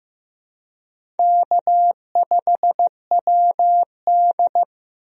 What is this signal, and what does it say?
Morse code, heard as a keyed tone, about 15 wpm, 700 Hz. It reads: K5WD
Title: USA Call Signs - 4 Characters - Repeated 3x - 15wpm